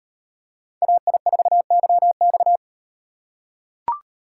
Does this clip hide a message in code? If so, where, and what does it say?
Yes: Morse code from 0.8 to 2.6 s, heard as a keyed tone, about 38 wpm, 700 Hz. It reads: AI4YX